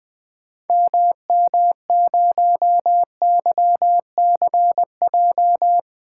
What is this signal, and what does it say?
Morse code, heard as a keyed tone, about 20 wpm, 700 Hz. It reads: MM0YCJ
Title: 2023 SOTA Activator Call Signs - 20wpm